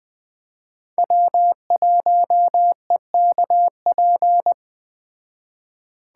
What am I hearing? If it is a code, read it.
Morse code, heard as a keyed tone, about 20 wpm, 700 Hz. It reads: W1EKP